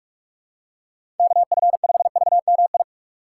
Morse code, heard as a keyed tone, about 46 wpm, 700 Hz. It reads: KF5VMI
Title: USA Call Signs - 6 Characters - 2x Character Spacing - 45wpm